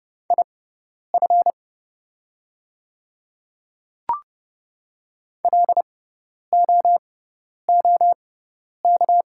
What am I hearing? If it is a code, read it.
Morse code, heard as a keyed tone, about 30 wpm, spaced as for 10 wpm, 700 Hz. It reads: IF LOOK
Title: 6x Character Spacing - Top 100 Words - 30wpm